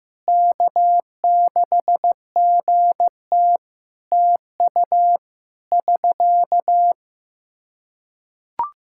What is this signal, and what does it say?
Morse code, heard as a keyed tone, about 15 wpm, 700 Hz. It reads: K6GT TU <SK>
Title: Practice QSO - Exchange 4 - Ending - 15wpm